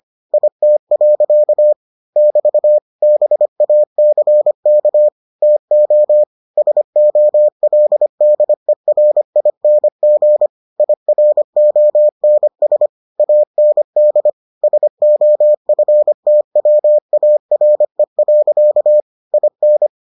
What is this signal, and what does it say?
Morse code, heard as a keyed tone, about 25 wpm, 600 Hz. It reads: IT. <BT> BACK TO SOLDERING IRONS AND SOFTWARE. IN